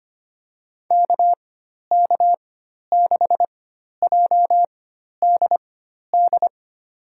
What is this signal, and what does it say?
Morse code, heard as a keyed tone, about 25 wpm, spaced as for 12 wpm, 700 Hz. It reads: KK6JDD
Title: USA Call Signs - 6 Characters - 4x Character Spacing - 25wpm